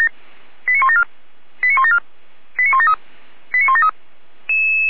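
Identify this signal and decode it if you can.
Five-tone selective call (ZVEI), heard as quick runs of five tones, tones of about 70 ms. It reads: 89174 89174 89173 89173
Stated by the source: ZVEI five-tone selective call received by SDR, from feuerwehr-orlamuende.org fire brigade audio archive.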